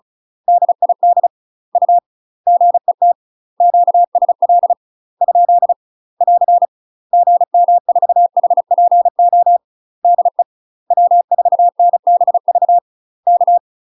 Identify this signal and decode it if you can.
Morse code, heard as a keyed tone, about 34 wpm, 700 Hz. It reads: DID U GET QSL ? <AR> GM4HPO DE W4NBV K